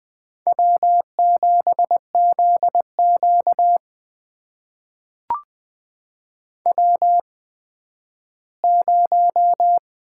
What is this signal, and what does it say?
Morse code, heard as a keyed tone, about 20 wpm, spaced as for 12 wpm, 700 Hz. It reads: W7ZQ W0
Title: USA Call Signs - 8x Character Spacing - 20wpm